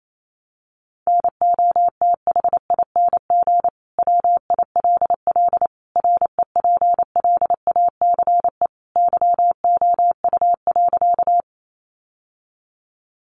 Morse code, heard as a keyed tone, about 28 wpm, 700 Hz. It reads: NOTHING WILL REPLACE YOU.
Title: Sentences from Top 1400 Words 28wpm